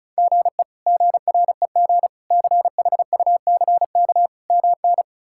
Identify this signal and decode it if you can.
Morse code, heard as a keyed tone, about 34 wpm, 700 Hz. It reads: GE GREG CHUCK MN
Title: K1USN SST QSO Practice - 35wpm